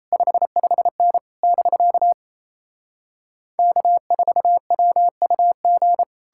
Morse code, heard as a keyed tone, about 37 wpm, 700 Hz. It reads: 55N <BK> K4WUG